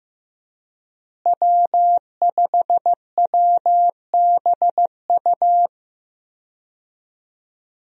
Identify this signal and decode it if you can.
Morse code, heard as a keyed tone, about 15 wpm, 700 Hz. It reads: W5WBU
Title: USA Call Signs - 5 Characters - Repeated 2x - 15wpm